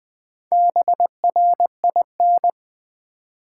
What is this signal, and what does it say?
Morse code, heard as a keyed tone, about 20 wpm, 700 Hz. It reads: BRIN
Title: Build A Word - 1 Letter At A Time - 5-Letter Words - 20wpm